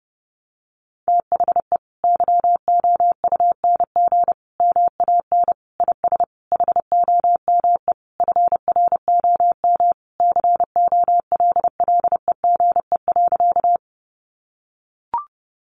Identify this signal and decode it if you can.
Morse code, heard as a keyed tone, about 30 wpm, 700 Hz. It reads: THE YOUNG MAN IS HOME FROM COLLEGE.
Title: Sentences from Top 1300 Words 30wpm